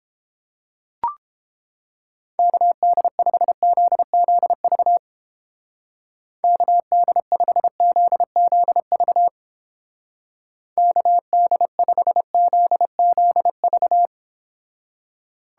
Morse code, heard as a keyed tone, about 29 wpm, 700 Hz. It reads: KD5ZZV KD5ZZV KD5ZZV